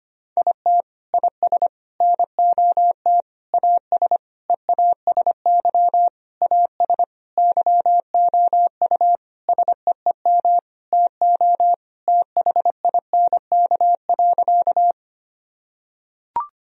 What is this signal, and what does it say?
Morse code, heard as a keyed tone, about 25 wpm, 700 Hz. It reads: IT IS NOT AS EASY AS YOU SEEM TO THINK.
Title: Sentences from Top 1000 Words - Review Part 1 - 25wpm